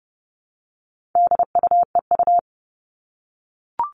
Morse code, heard as a keyed tone, about 30 wpm, 700 Hz. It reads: DUEU
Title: All Cut Numbers - 4-Digit 30wpm